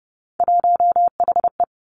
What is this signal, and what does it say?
Morse code, heard as a keyed tone, about 32 wpm, 700 Hz. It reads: 1HE
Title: Three Challenging Letter-Number Combinations 30wpm